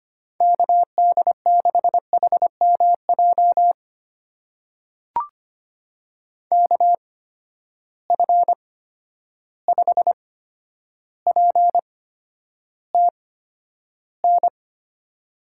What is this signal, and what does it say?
Morse code, heard as a keyed tone, about 25 wpm, spaced as for 11 wpm, 700 Hz. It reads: KD6HMJ KF5PTN
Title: USA Call Signs - 6 Characters - 8x Character Spacing - 25wpm